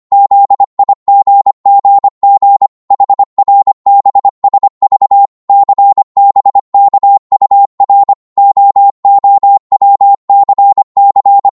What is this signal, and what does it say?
Morse code, heard as a keyed tone, about 25 wpm, 800 Hz. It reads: ZIGGG HRBSV CBKUR OOWCC
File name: Morse / ltr-25wpm